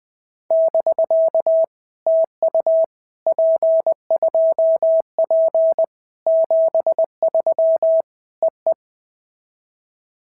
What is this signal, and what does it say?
Morse code, heard as a keyed tone, about 19 wpm, 650 Hz. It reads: <BK> TU P2P 73 EE
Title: POTA QSO Practice - 20wpm